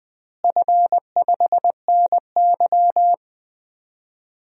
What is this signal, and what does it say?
Morse code, heard as a keyed tone, about 20 wpm, 700 Hz. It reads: F5NY
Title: DXCC-100 Call Signs - 4 Characters - Repeated 2x - 20wpm